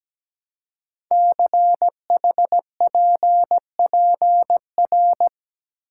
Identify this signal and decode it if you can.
Morse code, heard as a keyed tone, about 17 wpm, 700 Hz. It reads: CHPPR
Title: Code Groups - 5 Letters - 17wpm